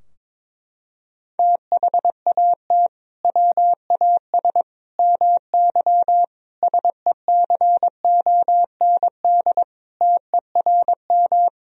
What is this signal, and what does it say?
Morse code, heard as a keyed tone, about 22 wpm, 700 Hz. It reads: THAT WAS MY SECOND TERM
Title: Sentences from Top 1000 Words - Review Part 4 - 22wpm